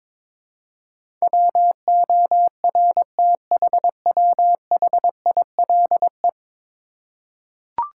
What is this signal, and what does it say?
Morse code, heard as a keyed tone, about 22 wpm, 700 Hz. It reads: WORTHWHILE
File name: practice / 22wpm / 10 Letter Words - First 3 Letters Given - 22wpm